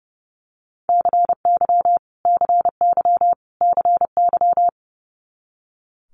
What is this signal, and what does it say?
Morse code, heard as a keyed tone, about 30 wpm, 700 Hz. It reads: CY CY CY
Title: Multi-letter Phonemes Repeated 3x 30wpm